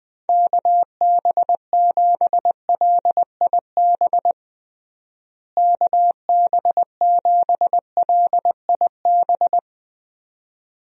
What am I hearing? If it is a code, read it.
Morse code, heard as a keyed tone, about 20 wpm, 700 Hz. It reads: KB7LIB KB7LIB
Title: USA Call Signs - 6 Characters - Repeated 2x - 20wpm